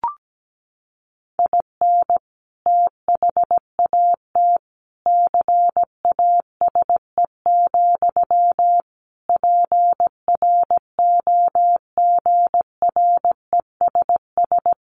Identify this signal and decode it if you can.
Morse code, heard as a keyed tone, about 17 wpm, 700 Hz. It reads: IN THAT CASE, PROGRESS
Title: Sentences from Top 1400 Words 17wpm